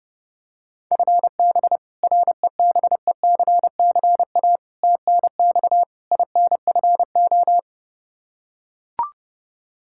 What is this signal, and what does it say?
Morse code, heard as a keyed tone, about 29 wpm, 700 Hz. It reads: FB REBECCA TNX INFO
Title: Practice QSO - Exchange 3 - More Info - 30wpm